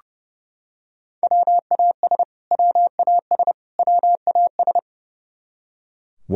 Morse code, heard as a keyed tone, about 32 wpm, 700 Hz. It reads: WAS WAS WAS
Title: Top 100 Words Repeated 30wpm